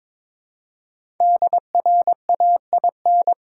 Morse code, heard as a keyed tone, about 22 wpm, 700 Hz. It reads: DRAIN
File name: 5 Letter Words - First 3 Letters Given - 22wpm